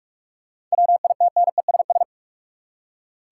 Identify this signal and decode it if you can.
Morse code, heard as a keyed tone, about 46 wpm, 700 Hz. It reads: WITNESS
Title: Seven Letter Words - 45wpm